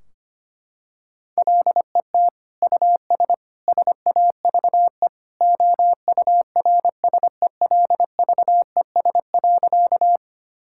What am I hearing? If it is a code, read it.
Morse code, heard as a keyed tone, about 25 wpm, 700 Hz. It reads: LET US SAVE OURSELVES.
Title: Sentences from Top 1600 Words 25wpm